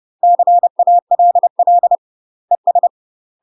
Morse code, heard as a keyed tone, about 30 wpm, 700 Hz. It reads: CALL ES
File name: UK MASTER QSOs TRAINING / 30WPM / UK MASTER QSOs 0013